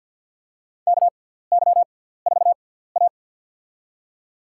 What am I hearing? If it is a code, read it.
Morse code, heard as a keyed tone, about 48 wpm, spaced as for 17 wpm, 700 Hz. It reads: KY4A
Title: USA Call Signs - 6x Character Spacing - 50wpm